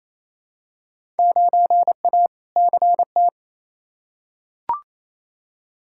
Morse code, heard as a keyed tone, about 28 wpm, 700 Hz. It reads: 9A CT
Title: Field Day Exchanges Encore - 28wpm